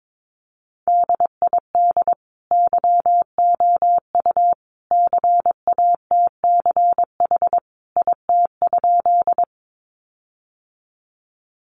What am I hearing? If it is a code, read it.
Morse code, heard as a keyed tone, about 22 wpm, 700 Hz. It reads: DID YOU CATCH IT?
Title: Sentences from Top 1500 Words 22wpm